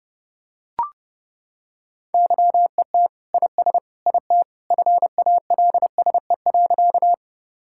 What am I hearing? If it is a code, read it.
Morse code, heard as a keyed tone, about 30 wpm, 700 Hz. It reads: YET IS IT FALSE.
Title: Sentences from Top 1600 Words 30wpm